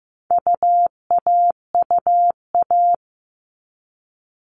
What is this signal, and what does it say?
Morse code, heard as a keyed tone, about 15 wpm, 700 Hz. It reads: UAUA
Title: All Cut Numbers - 4-Digit 15wpm